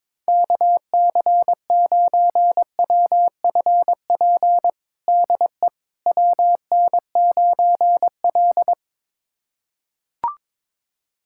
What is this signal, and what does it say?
Morse code, heard as a keyed tone, about 22 wpm, 700 Hz. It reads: KC9WFP DE WN9L
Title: USA Callsign DE Callsign 22wpm